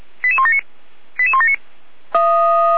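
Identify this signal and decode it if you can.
Five-tone selective call (ZVEI), heard as quick runs of five tones, tones of about 70 ms. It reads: 88179 88179
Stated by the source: ZVEI five-tone selective call received on an SDR.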